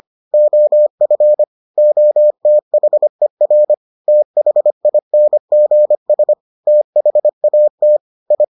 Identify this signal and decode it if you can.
Morse code, heard as a keyed tone, about 25 wpm, 600 Hz. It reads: OF OTHER THINGS THAT I